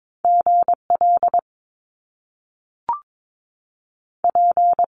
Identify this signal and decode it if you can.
Morse code, heard as a keyed tone, about 22 wpm, spaced as for 3 wpm, 700 Hz. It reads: G LP
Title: Top Two Letter Combinations in Words 22wpm